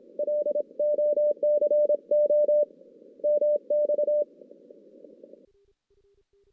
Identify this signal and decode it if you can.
Morse code, heard as a keyed tone, about 26 wpm, 590 Hz. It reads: LOCO MX
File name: 2016 JAN CW NAQP / 2016 Jan naqp k5ot - 0-6